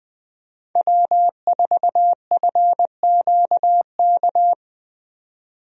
Morse code, heard as a keyed tone, about 20 wpm, 700 Hz. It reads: W4FQK
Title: USA Call Signs - 5 Characters - 20wpm